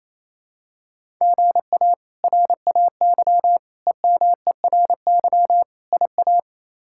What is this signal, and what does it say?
Morse code, heard as a keyed tone, about 28 wpm, 700 Hz. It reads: GA RAY EMERY IA
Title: K1USN SST QSO Practice - 28wpm